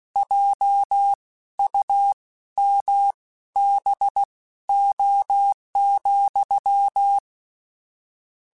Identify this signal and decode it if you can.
Morse code, heard as a keyed tone, about 16 wpm, spaced as for 12 wpm, 800 Hz. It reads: JUMBO,